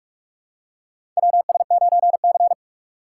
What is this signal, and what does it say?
Morse code, heard as a keyed tone, about 43 wpm, 700 Hz. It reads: WS9C
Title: USA Call Signs - 4 Characters - 2x Character Spacing - 45wpm